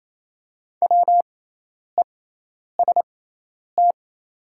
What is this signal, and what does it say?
Morse code, heard as a keyed tone, about 28 wpm, spaced as for 10 wpm, 700 Hz. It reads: WEST
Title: 6x Character Spacing - Top 500 Words - 28wpm